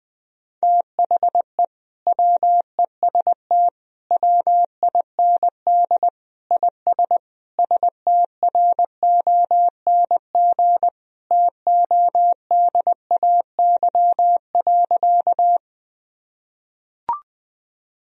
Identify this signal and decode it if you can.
Morse code, heard as a keyed tone, about 20 wpm, 700 Hz. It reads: THE WEST WIND IS STRONG TODAY.
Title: Sentences from Top 1000 Words - Review Part 2 - 20wpm